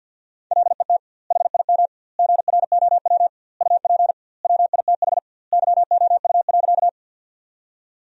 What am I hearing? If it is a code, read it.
Morse code, heard as a keyed tone, about 50 wpm, 700 Hz. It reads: LET HIM GROW UP WITH YOU.
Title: 2x Word-Spacing - Sentences from Top 300 Words - 50wpm